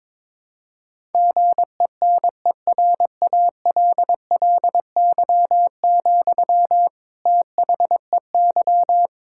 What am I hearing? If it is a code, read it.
Morse code, heard as a keyed tone, about 22 wpm, 700 Hz. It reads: GENERALLY, THEY
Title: Sentences from Top 1000 Words - Review Part 6 - 22wpm